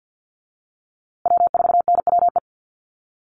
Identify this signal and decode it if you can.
Morse code, heard as a keyed tone, about 50 wpm, 700 Hz. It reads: W4NWE
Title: USA Call Signs Repeated 2x 50wpm